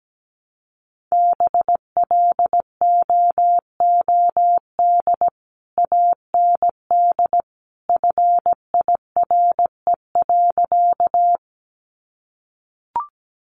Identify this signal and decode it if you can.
Morse code, heard as a keyed tone, about 17 wpm, 700 Hz. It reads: BLOOD AND FIRE.
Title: Sentences from Top 1100 Words 17wpm